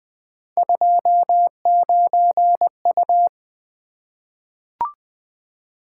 Morse code, heard as a keyed tone, about 20 wpm, 700 Hz. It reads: 29U
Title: Three Letter-Number Combinations - 20wpm